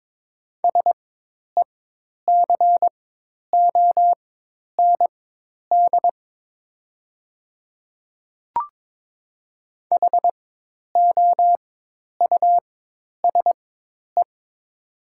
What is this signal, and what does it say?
Morse code, heard as a keyed tone, about 22 wpm, spaced as for 10 wpm, 700 Hz. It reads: SECOND HOUSE
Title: 4x Character Spacing - Top 300 Words - 22wpm